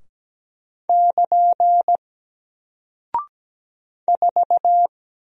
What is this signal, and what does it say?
Morse code, heard as a keyed tone, about 17 wpm, 700 Hz. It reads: <KN> 4
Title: Single Character (Letter:Number:Punctuation:Prosign) 17wpm